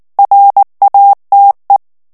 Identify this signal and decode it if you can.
Morse code, heard as a keyed tone, about 19 wpm, 800 Hz. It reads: RATE
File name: morse mp3 / ANGLAIS texte clair 3 vitesse 18 mots minute